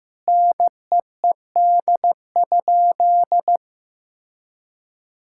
Morse code, heard as a keyed tone, about 15 wpm, 700 Hz. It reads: NEED?
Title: Sentences from Top 1000 Words - Review Part 5 - 15wpm